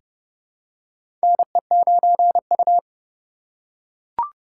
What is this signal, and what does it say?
Morse code, heard as a keyed tone, about 29 wpm, 700 Hz. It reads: NE9U
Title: K1USN SST QSO Practice - 30wpm